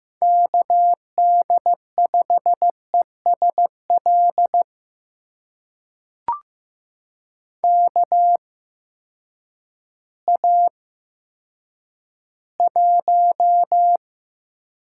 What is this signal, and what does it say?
Morse code, heard as a keyed tone, about 15 wpm, spaced as for 9 wpm, 700 Hz. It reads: KD5ESL KA1
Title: USA Call Signs - 6 Characters - 8x Character Spacing - 15wpm